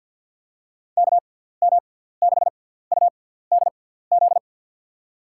Morse code, heard as a keyed tone, about 49 wpm, spaced as for 17 wpm, 700 Hz. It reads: KM6UDZ
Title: USA Call Signs - 6x Character Spacing - 50wpm